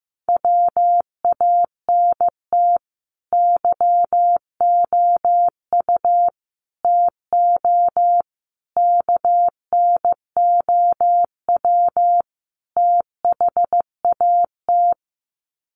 Morse code, heard as a keyed tone, about 15 wpm, 700 Hz. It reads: WANT YOU TO KNOW THAT
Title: Sets of 5 Words Encore 15wpm